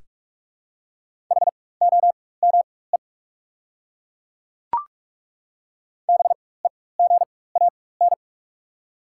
Morse code, heard as a keyed tone, about 48 wpm, spaced as for 21 wpm, 700 Hz. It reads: HOME BEGAN
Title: 4x Character Spacing - Top 300 Words - 45wpm